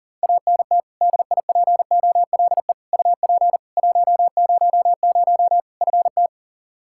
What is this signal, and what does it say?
Morse code, heard as a keyed tone, about 40 wpm, 700 Hz. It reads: ANT DIPOLE UP 100 FT
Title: Practice QSO - Exchange 2 - Info - 40wpm